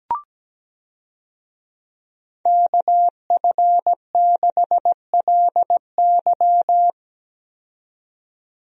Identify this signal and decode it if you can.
Morse code, heard as a keyed tone, about 17 wpm, 700 Hz. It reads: KF6LY
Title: USA Call Signs - 5 Characters - 17wpm